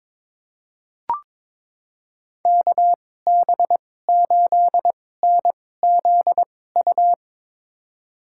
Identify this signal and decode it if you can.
Morse code, heard as a keyed tone, about 21 wpm, spaced as for 16 wpm, 700 Hz. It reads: KB8NZU
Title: USA Call Signs - 2x Character Spacing - 22wpm